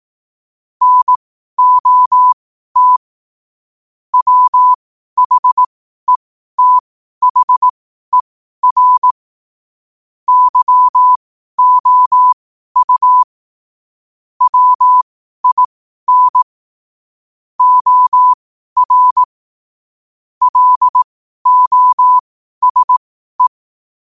Morse code, heard as a keyed tone, about 18 wpm, spaced as for 12 wpm, 1000 Hz. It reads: NOT WHETHER YOU WIN OR LOSE
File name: qotdfiles / 12wpm